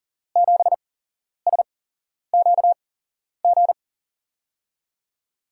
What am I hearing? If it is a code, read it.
Morse code, heard as a keyed tone, about 39 wpm, spaced as for 11 wpm, 700 Hz. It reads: 7SQG
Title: USA Call Signs - 6 Characters - 8x Character Spacing - 40wpm